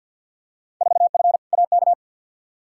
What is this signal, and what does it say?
Morse code, heard as a keyed tone, about 48 wpm, 700 Hz. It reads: 4F AK
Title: Field Day Exchanges Encore - 50wpm